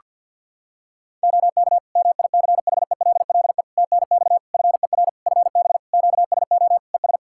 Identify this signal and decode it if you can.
Morse code, heard as a keyed tone, about 51 wpm, 700 Hz. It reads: OK MICHELLE TNX FER FB QSO ES
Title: Practice QSO - Exchange 4 - Ending - 50wpm